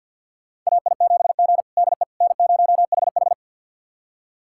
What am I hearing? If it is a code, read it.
Morse code, heard as a keyed tone, about 52 wpm, 700 Hz. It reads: AI7G DE N0HH
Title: USA Callsign DE Callsign - 4 Characters - 50wpm